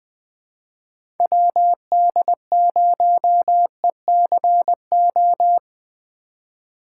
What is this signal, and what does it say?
Morse code, heard as a keyed tone, about 20 wpm, 700 Hz. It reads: WD0ECO
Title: K1USN SST QSO Practice - 20wpm